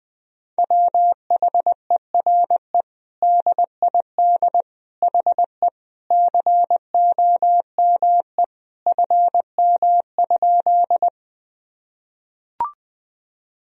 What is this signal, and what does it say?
Morse code, heard as a keyed tone, about 20 wpm, 700 Hz. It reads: WHERE DID HE COME FM?
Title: Sentences from Top 200 Words and Abbreviations 20wpm